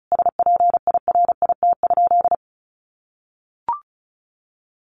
Morse code, heard as a keyed tone, about 34 wpm, 700 Hz. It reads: SPIRIT?
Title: Sentences from Top 1400 Words 35wpm